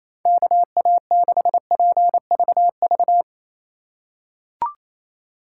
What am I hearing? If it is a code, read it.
Morse code, heard as a keyed tone, about 28 wpm, 700 Hz. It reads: KA6PVV
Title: USA Call Signs - 3x Character Spacing - 28wpm